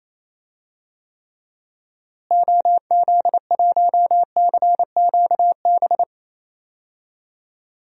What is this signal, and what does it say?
Morse code, heard as a keyed tone, about 28 wpm, 700 Hz. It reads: OZ1CQB
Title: DXCC-100 Call Signs - 6 Characters - Repeated 3x - 28wpm